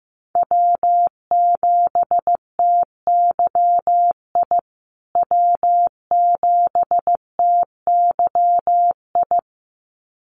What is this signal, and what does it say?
Morse code, heard as a keyed tone, about 14 wpm, 700 Hz. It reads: W7TYI W7TYI
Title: USA Call Signs Repeated 2x 15wpm